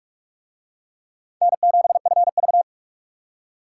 Morse code, heard as a keyed tone, about 39 wpm, 700 Hz. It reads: N7FV